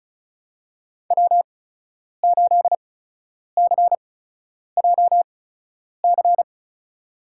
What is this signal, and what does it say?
Morse code, heard as a keyed tone, about 34 wpm, spaced as for 10 wpm, 700 Hz. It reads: W8CJC
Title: USA Call Signs - 8x Character Spacing - 35wpm